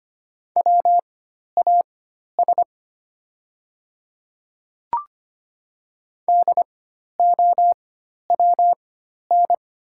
Morse code, heard as a keyed tone, about 25 wpm, spaced as for 12 wpm, 700 Hz. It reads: WAS DOWN
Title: 4x Character Spacing - Top 100 Words - 25wpm